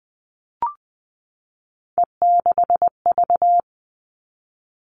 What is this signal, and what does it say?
Morse code, heard as a keyed tone, about 20 wpm, 700 Hz. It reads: E6V
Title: All Cut Numbers - 3-Digit 20wpm